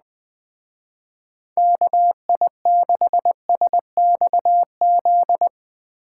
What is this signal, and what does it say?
Morse code, heard as a keyed tone, about 20 wpm, 700 Hz. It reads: KI6SXZ
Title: Speed Racing - USA Call Signs 17wpm